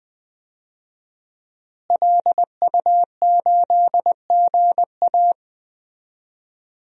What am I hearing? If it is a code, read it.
Morse code, heard as a keyed tone, about 20 wpm, 700 Hz. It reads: LU8GA
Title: DXCC-100 Call Signs - 5 Characters - Repeated 2x - 20wpm